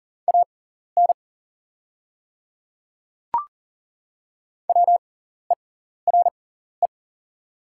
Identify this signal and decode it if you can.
Morse code, heard as a keyed tone, about 40 wpm, spaced as for 14 wpm, 690 Hz. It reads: AN WERE
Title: 6x Character Spacing - Top 100 Words - 40wpm